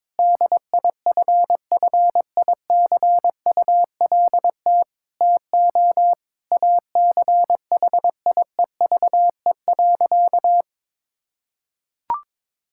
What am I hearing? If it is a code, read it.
Morse code, heard as a keyed tone, about 22 wpm, 700 Hz. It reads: DIFFICULT TO ACHIEVE.
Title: Sentences from Top 1000 Words - Review Part 6 - 22wpm